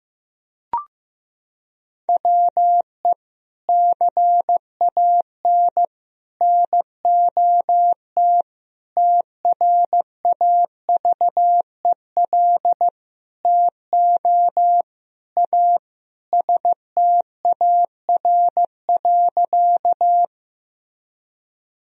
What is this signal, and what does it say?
Morse code, heard as a keyed tone, about 15 wpm, 700 Hz. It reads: WE CAN NOT TRAVEL TO A STAR.
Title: Sentences from Top 1000 Words - Review Part 2 - 15wpm